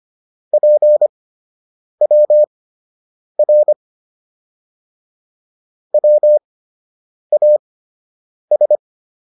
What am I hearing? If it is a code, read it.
Morse code, heard as a keyed tone, about 25 wpm, spaced as for 8 wpm, 600 Hz. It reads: PWR WAS